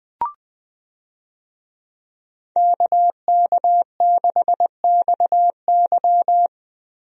Morse code, heard as a keyed tone, about 21 wpm, 700 Hz. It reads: KK6XY
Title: USA Call Signs - 5 Characters - Repeated 2x - 20wpm